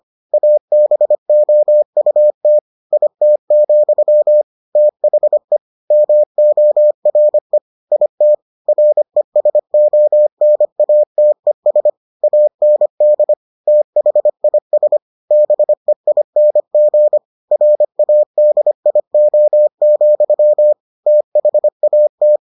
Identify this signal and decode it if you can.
Morse code, heard as a keyed tone, about 25 wpm, 600 Hz. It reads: ABOUT IT, THE MORE IT RESONATES AND THIS BEING RADIO, THAT